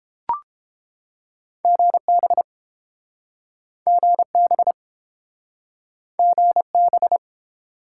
Morse code, heard as a keyed tone, about 32 wpm, 700 Hz. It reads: GB GB GB